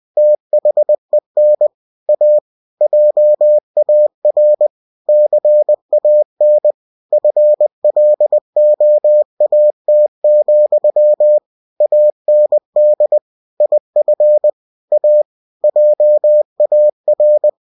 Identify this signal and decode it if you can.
Morse code, heard as a keyed tone, about 20 wpm, 600 Hz. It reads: THEN A JAR CAN FLOAT, AND IF A JAR
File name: cw-books / winnie-the-pooh / winnie-20wpm-page-0095